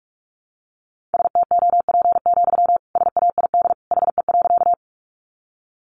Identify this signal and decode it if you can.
Morse code, heard as a keyed tone, about 44 wpm, 700 Hz. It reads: STOP, SAID HE.
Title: Sentences from Top 300 Words 45wpm